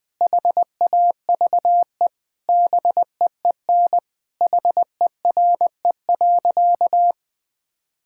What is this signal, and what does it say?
Morse code, heard as a keyed tone, about 20 wpm, 700 Hz. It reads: HAVE BEEN HERE.
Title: Sentences with 4-Word Phrases - 20wpm